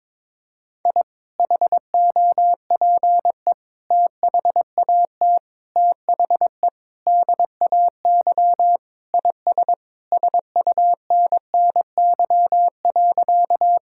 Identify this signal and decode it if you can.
Morse code, heard as a keyed tone, about 22 wpm, 700 Hz. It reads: I HOPE THAT THE DAY IS SUNNY.
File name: practice / 22wpm / Sentences with 2-Word Phrases - 22wpm